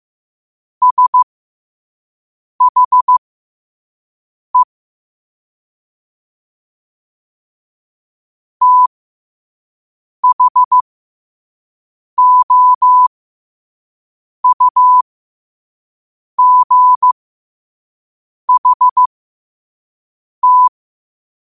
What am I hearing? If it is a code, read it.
Morse code, heard as a keyed tone, about 15 wpm, spaced as for 5 wpm, 1000 Hz. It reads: SHE THOUGHT
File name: qotdfiles / 5wpm